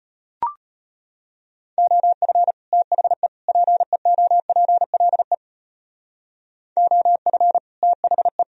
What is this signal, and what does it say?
Morse code, heard as a keyed tone, about 32 wpm, 700 Hz. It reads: OF THE PEOPLE OF THE